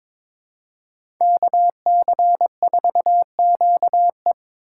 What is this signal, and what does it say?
Morse code, heard as a keyed tone, about 22 wpm, 700 Hz. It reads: KC4QE
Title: USA Call Signs - 3x Character Spacing - 22wpm